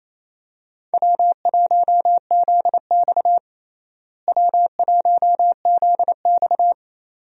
Morse code, heard as a keyed tone, about 28 wpm, 700 Hz. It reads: W1ZX W1ZX
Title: USA Call Signs - 4 Characters - Repeated 2x - 28wpm